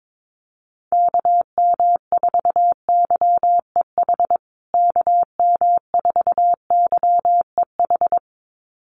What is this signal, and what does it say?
Morse code, heard as a keyed tone, about 22 wpm, 700 Hz. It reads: KM4YEH KM4YEH
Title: USA Call Signs Repeated 2x 22wpm